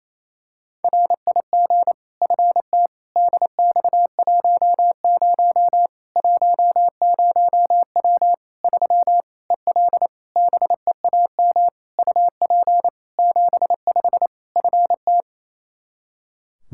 Morse code, heard as a keyed tone, about 28 wpm, 700 Hz. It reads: RIG FT DX10 10W 3 EL BEAM UP 75 FT
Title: K1USN SST QSO Practice 28wpm